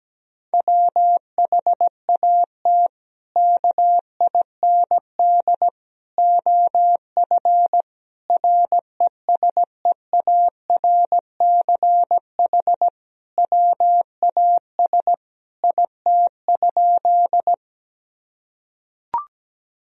Morse code, heard as a keyed tone, about 17 wpm, 700 Hz. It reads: WHAT KIND OF RESEARCH WAS IT?
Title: Sentences from Top 1000 Words - Review Part 4 - 17wpm